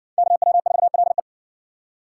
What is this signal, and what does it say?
Morse code, heard as a keyed tone, about 60 wpm, 700 Hz. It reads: KW4LE